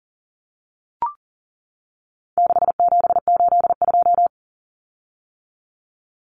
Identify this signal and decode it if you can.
Morse code, heard as a keyed tone, about 40 wpm, 700 Hz. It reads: 6782
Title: Common Cut Numbers - 4-Digit 40wpm